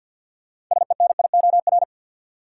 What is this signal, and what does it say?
Morse code, heard as a keyed tone, about 47 wpm, 700 Hz. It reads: SENIOR
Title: Sets of 5 Words Encore - New Word Drill - 50wpm